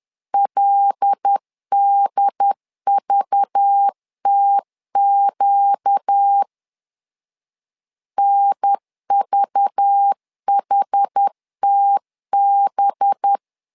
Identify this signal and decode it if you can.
Morse code, heard as a keyed tone, about 10 wpm, 800 Hz. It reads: LDVTQ NVHTB
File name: LEZIONI / 5 LBVC / 5 LBV 10wpm 2248